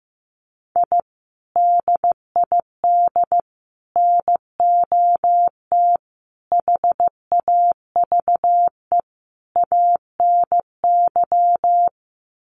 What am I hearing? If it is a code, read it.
Morse code, heard as a keyed tone, about 15 wpm, 700 Hz. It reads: I DID NOT HAVE ANY
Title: Sets of 5 Words Encore 15wpm